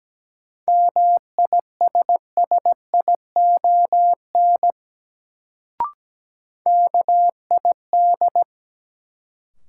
Morse code, heard as a keyed tone, about 17 wpm, 700 Hz. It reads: MISSION KID
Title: Sentences from Top 1600 Words - New Word Drill Alt - 17wpm